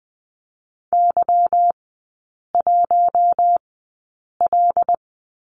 Morse code, heard as a keyed tone, about 20 wpm, spaced as for 8 wpm, 700 Hz. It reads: Y1L
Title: 191 Lesson 31e - Letter v - Three Characters 20wpm